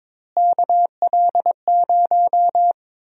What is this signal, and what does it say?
Morse code, heard as a keyed tone, about 22 wpm, 700 Hz. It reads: KL0
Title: USA Call Sign Prefixes - Repeated 2x - 22wpm